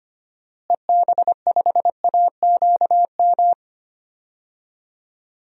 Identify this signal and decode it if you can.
Morse code, heard as a keyed tone, about 25 wpm, 700 Hz. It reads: EB5AQM
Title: DXCC-100 Call Signs - 6 Characters - Repeated 3x - 25wpm